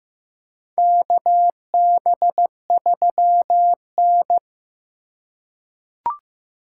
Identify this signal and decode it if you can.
Morse code, heard as a keyed tone, about 15 wpm, 700 Hz. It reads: KB3N
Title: USA Call Signs - 4 Characters - 4x Character Spacing - 15wpm